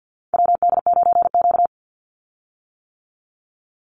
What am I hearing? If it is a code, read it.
Morse code, heard as a keyed tone, about 50 wpm, 700 Hz. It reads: WD9Q
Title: Call Signs - 50wpm